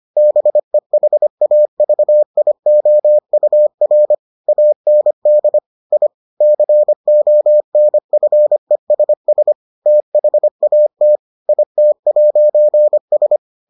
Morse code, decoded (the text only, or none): BEHAVIOUR AND I CONFESS THAT IT'S